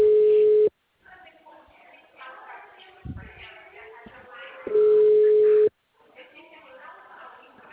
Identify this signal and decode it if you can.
Ringback tone (Europe, 425 Hz), with speech.